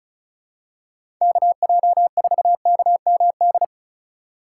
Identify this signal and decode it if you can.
Morse code, heard as a keyed tone, about 36 wpm, 700 Hz. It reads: KJ4KMD
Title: USA Call Signs - 2x Character Spacing - 35wpm